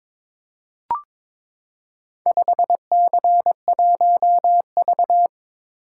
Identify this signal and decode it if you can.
Morse code, heard as a keyed tone, about 22 wpm, 700 Hz. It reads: 5C1V
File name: practice / 22wpm / Four Letter-Number Combinations - 22wpm